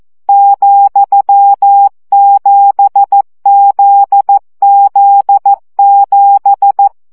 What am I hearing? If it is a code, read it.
Morse code, heard as a keyed tone, about 14 wpm, 800 Hz. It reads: ,7ZZ7
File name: CW gyakorlas z 7 vesszo 12 baud